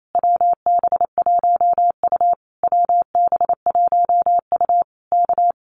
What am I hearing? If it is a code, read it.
Morse code, heard as a keyed tone, about 28 wpm, 700 Hz. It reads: WB1U WB1U K
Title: Common QSO Phrases Encore 28wpm